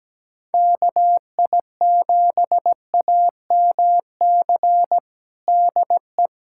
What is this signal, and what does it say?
Morse code, heard as a keyed tone, about 17 wpm, 700 Hz. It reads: KI7AMC DE
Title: USA Callsign DE Callsign 17wpm